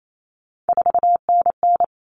Morse code, heard as a keyed tone, about 28 wpm, 700 Hz. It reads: 4NN
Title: All Cut Numbers - 3-Digit 28wpm